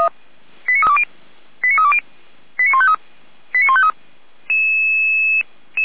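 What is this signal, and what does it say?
Five-tone selective call (ZVEI), heard as quick runs of five tones, tones of about 70 ms. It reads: 89320 89320 89153 89153
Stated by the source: Five-tone ZVEI selective call received by SDR.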